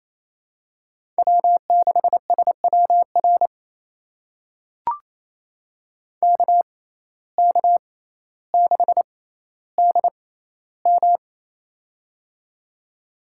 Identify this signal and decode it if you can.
Morse code, heard as a keyed tone, about 28 wpm, spaced as for 14 wpm, 700 Hz. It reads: W6SWR KK6DM
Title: USA Call Signs - 5 Characters - 6x Character Spacing - 28wpm